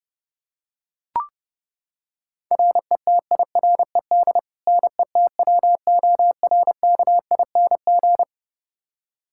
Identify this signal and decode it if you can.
Morse code, heard as a keyed tone, about 30 wpm, 700 Hz. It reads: RETIRED NETWORKING
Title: K1USN SST QSO Practice 30wpm